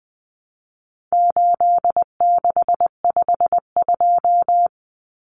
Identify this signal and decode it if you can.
Morse code, heard as a keyed tone, about 20 wpm, 700 Hz. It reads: 8652
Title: Common Cut Numbers - 4-Digit 20wpm